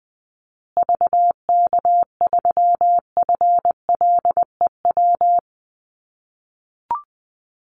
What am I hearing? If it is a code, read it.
Morse code, heard as a keyed tone, about 20 wpm, 700 Hz. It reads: VK3FLEW
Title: USA Call Signs Repeated 3x 20wpm